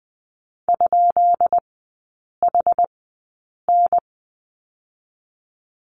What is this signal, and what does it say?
Morse code, heard as a keyed tone, about 20 wpm, spaced as for 8 wpm, 700 Hz. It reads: ?HN
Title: 133 Lesson 22e - Question Mark - Three Characters 20wpm